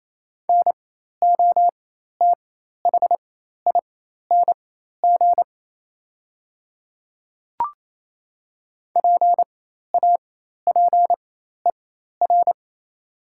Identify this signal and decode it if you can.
Morse code, heard as a keyed tone, about 28 wpm, spaced as for 13 wpm, 700 Hz. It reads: NOTHING PAPER
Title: 4x Character Spacing - Top 500 Words - 28wpm